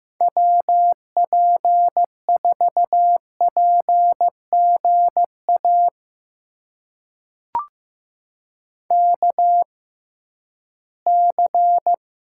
Morse code, heard as a keyed tone, about 15 wpm, spaced as for 11 wpm, 700 Hz. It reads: WP4PGA KC
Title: USA Call Signs - 6 Characters - 6x Character Spacing - 15wpm